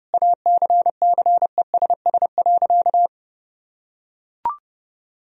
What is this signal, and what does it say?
Morse code, heard as a keyed tone, about 30 wpm, 700 Hz. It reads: ACCESS.
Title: Sentences from Top 1000 Words - Review Part 4 - 30wpm